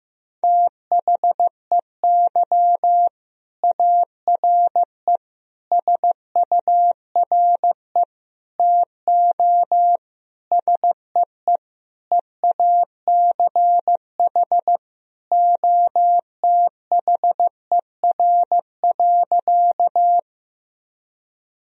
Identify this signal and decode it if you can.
Morse code, heard as a keyed tone, about 15 wpm, 700 Hz. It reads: THEY ARE SURE TO SEE EACH OTHER.
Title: Sentences from Top 1000 Words - Review Part 1 - 15wpm